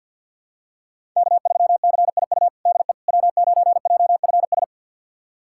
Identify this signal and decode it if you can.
Morse code, heard as a keyed tone, about 53 wpm, 690 Hz. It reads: K3CIU DE W9JFS
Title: USA Callsign DE Callsign - 5 Characters - 50wpm